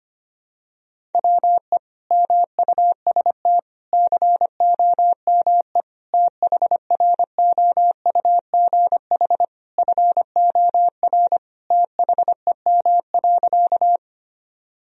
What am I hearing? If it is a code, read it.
Morse code, heard as a keyed tone, about 25 wpm, 700 Hz. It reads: WE MUST COME THROUGH FOR THEM.
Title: Sentences from Top 1000 Words - Review Part 1 - 25wpm